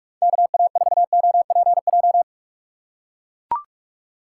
Morse code, heard as a keyed tone, about 43 wpm, 700 Hz. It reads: KA4OPJ
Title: USA Call Signs - 6 Characters - 3x Character Spacing - 45wpm